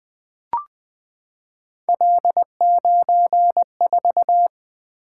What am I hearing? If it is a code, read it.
Morse code, heard as a keyed tone, about 20 wpm, 700 Hz. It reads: L94
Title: Three Letter-Number Combinations - 20wpm